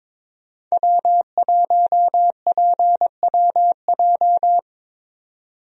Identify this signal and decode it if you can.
Morse code, heard as a keyed tone, about 22 wpm, 700 Hz. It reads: W1PWJ
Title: USA Call Signs - 5 Characters - 22wpm